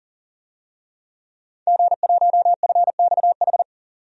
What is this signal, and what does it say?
Morse code, heard as a keyed tone, about 40 wpm, 700 Hz. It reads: G1FXH
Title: DXCC-100 Call Signs - 5 Characters - 40wpm